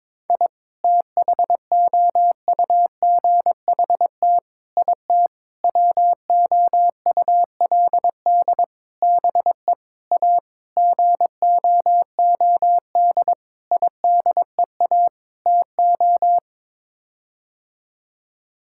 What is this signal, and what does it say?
Morse code, heard as a keyed tone, about 22 wpm, 700 Hz. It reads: I THOUGHT IT WOULD BE A GOOD IDEA TO
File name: Sets of 9 Words - 22wpm